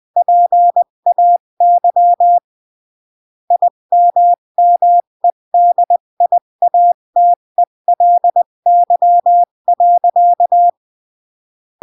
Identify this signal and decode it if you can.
Morse code, heard as a keyed tone, about 20 wpm, 700 Hz. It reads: PAY IMMEDIATELY.
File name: CW / jodorowsky / 82 commandments 20-15